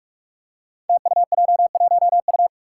Morse code, heard as a keyed tone, about 44 wpm, 700 Hz. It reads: TUJ1U
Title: Five Challenging Letter-Number Combinations 45wpm